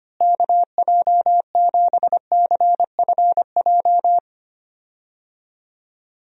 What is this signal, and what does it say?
Morse code, heard as a keyed tone, about 25 wpm, 700 Hz. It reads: KJ7CFJ
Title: USA Call Signs - 6 Characters - Repeated 2x - 25wpm